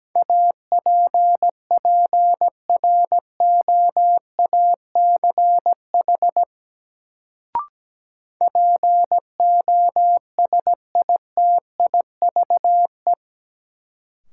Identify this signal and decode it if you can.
Morse code, heard as a keyed tone, about 17 wpm, 700 Hz. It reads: APPROACH POSITIVE
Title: Sentences from Top 1400 Words - New Word Drill Alt - 17wpm